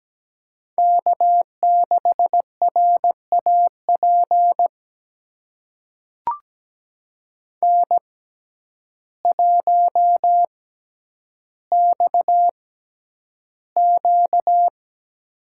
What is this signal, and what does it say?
Morse code, heard as a keyed tone, about 17 wpm, spaced as for 9 wpm, 700 Hz. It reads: K6RAP N1XQ
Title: USA Call Signs - 6x Character Spacing - 17wpm